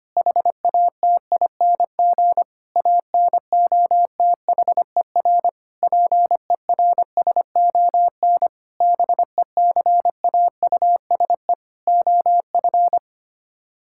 Morse code, heard as a keyed tone, about 25 wpm, 700 Hz. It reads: HATING ANOTHER PERSON BECAUSE OF
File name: Sets of 9 Words - 25wpm